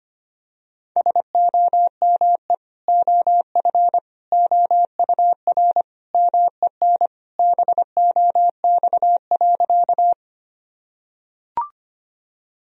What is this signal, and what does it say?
Morse code, heard as a keyed tone, about 25 wpm, 700 Hz. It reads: SOME OF OUR MEN BOX.
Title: Sentences from Top 1000 Words - Review Part 2 - 25wpm